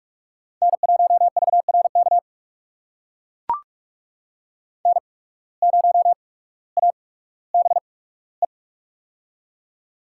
Morse code, heard as a keyed tone, about 44 wpm, spaced as for 19 wpm, 700 Hz. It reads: N1VRK N0ABE
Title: USA Call Signs - 5 Characters - 8x Character Spacing - 45wpm